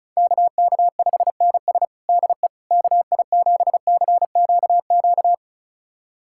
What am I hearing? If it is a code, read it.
Morse code, heard as a keyed tone, about 36 wpm, 700 Hz. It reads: KK5NS DE KI7CQQ
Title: USA Callsign DE Callsign 35wpm